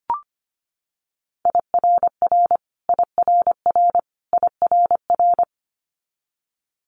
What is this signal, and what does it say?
Morse code, heard as a keyed tone, about 25 wpm, 700 Hz. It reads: IRR IRR IRR
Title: Multi-letter Phonemes Repeated 3x 25wpm